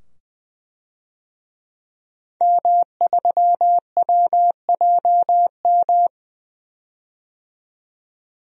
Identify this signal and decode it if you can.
Morse code, heard as a keyed tone, about 20 wpm, 700 Hz. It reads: M3WJM
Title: DXCC-100 Call Signs - 5 Characters - 20wpm